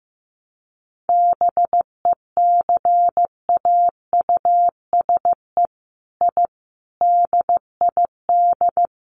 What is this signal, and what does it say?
Morse code, heard as a keyed tone, about 15 wpm, 700 Hz. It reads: BECAUSE I DID
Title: Sets of 5 Words Encore2 15wpm